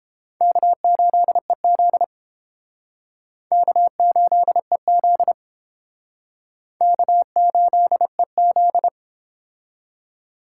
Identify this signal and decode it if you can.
Morse code, heard as a keyed tone, about 30 wpm, 700 Hz. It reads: K8EZ K8EZ K8EZ